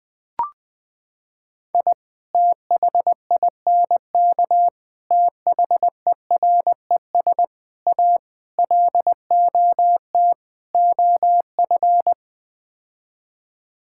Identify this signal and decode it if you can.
Morse code, heard as a keyed tone, about 20 wpm, 700 Hz. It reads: I THINK THERES A LOT OF
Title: Sets of 6 Words - 20wpm